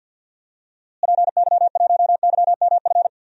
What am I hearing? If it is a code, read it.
Morse code, heard as a keyed tone, about 47 wpm, 700 Hz. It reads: PY1YMF
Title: DXCC-100 Call Signs - 6 Characters - Repeated 2x - 50wpm